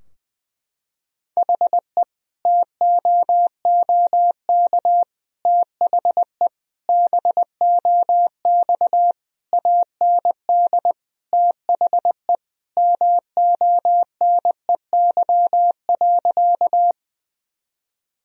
Morse code, heard as a keyed tone, about 21 wpm, 700 Hz. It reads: HE TOOK THE BOX AND THE MONEY.
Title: Sentences from Top 1000 Words - Review Part 2 - 20wpm